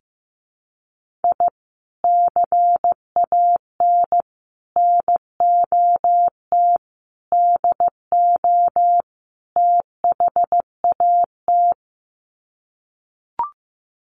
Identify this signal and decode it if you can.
Morse code, heard as a keyed tone, about 15 wpm, 700 Hz. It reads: I CAN NOT DO THAT
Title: Sets of 5 Words Encore2 15wpm